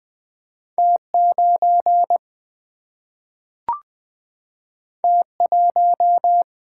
Morse code, heard as a keyed tone, about 20 wpm, 700 Hz. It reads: T 9 T 1
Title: Single-Letter Single-Number 20wpm